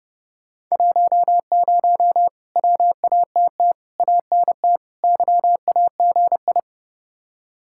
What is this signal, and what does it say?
Morse code, heard as a keyed tone, about 37 wpm, 700 Hz. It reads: 10 WATT ANT YAGI
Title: Speed-Racing Common QSO Phrases Encore - 30wpm